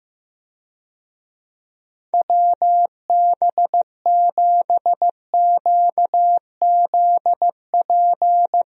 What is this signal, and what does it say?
Morse code, heard as a keyed tone, about 15 wpm, 700 Hz. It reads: WB7QZP
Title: USA Call Signs - 6 Characters - Repeated 2x - 15wpm